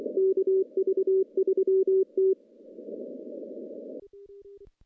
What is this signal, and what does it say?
Morse code, heard as a keyed tone, about 24 wpm, 380 Hz. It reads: KV3T
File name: CW-SS-k5tr